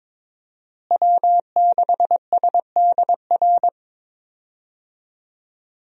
Morse code, heard as a keyed tone, about 22 wpm, 700 Hz. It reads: W6SDR
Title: USA Call Signs - 5 Characters - Repeated 2x - 22wpm